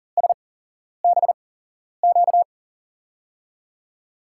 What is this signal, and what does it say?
Morse code, heard as a keyed tone, about 40 wpm, spaced as for 11 wpm, 700 Hz. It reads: SBQ
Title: USA Call Signs - 5 Characters - 8x Character Spacing - 40wpm